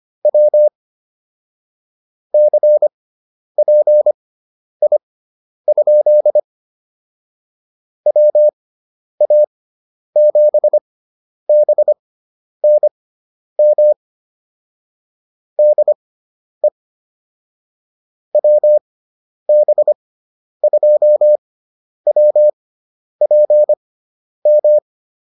Morse code, heard as a keyed tone, about 25 wpm, spaced as for 10 wpm, 600 Hz. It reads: W CPI? WA7BNM DE WB2WPM